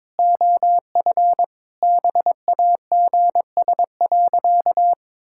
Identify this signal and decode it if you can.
Morse code, heard as a keyed tone, about 22 wpm, 700 Hz. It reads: OF BAGS.
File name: practice / 22wpm / Sentences with 4-Word Phrases - 22wpm